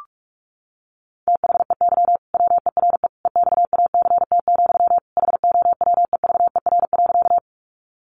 Morse code, heard as a keyed tone, about 46 wpm, 700 Hz. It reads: THEY WERE EXACT, HOWEVER.
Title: Sentences from Top 1300 Words 45wpm